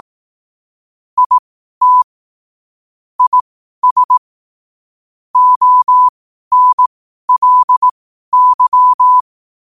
Morse code, heard as a keyed tone, about 18 wpm, spaced as for 12 wpm, 1000 Hz. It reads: IT IS ONLY
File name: qotdfiles / 12wpm